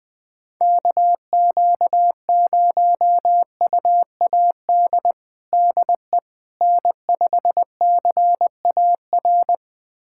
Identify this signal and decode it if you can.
Morse code, heard as a keyed tone, about 20 wpm, 700 Hz. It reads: KQ0UAD DE N5CAR
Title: USA Callsign DE Callsign 20wpm